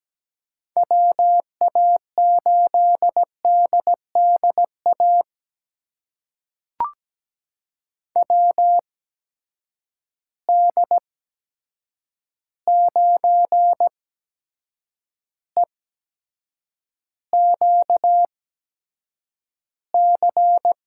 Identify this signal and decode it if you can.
Morse code, heard as a keyed tone, about 17 wpm, spaced as for 7 wpm, 700 Hz. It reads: WA8DDA WD9EQC
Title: USA Call Signs - 6 Characters - 8x Character Spacing - 17wpm